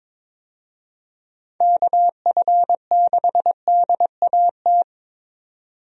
Morse code, heard as a keyed tone, about 22 wpm, 700 Hz. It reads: KF6DAT
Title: USA Call Signs - 6 Characters - 22wpm